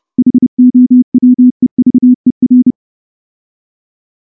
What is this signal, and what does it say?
Morse code, heard as a keyed tone, about 29 wpm, 250 Hz. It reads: HOWEVER
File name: morse / morse-training-30WPM